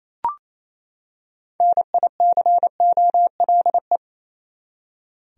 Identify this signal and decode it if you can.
Morse code, heard as a keyed tone, about 28 wpm, 700 Hz. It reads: NICOLE
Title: USA Common Names - 28wpm